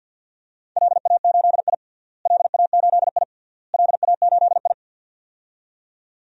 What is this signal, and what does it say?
Morse code, heard as a keyed tone, about 47 wpm, 700 Hz. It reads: LA8I LA8I LA8I
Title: DXCC-100 Call Signs - 4 Characters - Repeated 3x - 50wpm